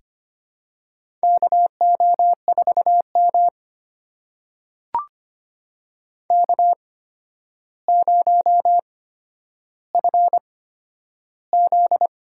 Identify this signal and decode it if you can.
Morse code, heard as a keyed tone, about 25 wpm, spaced as for 11 wpm, 700 Hz. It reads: KO4M K0FZ
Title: USA Call Signs - 4 Characters - 8x Character Spacing - 25wpm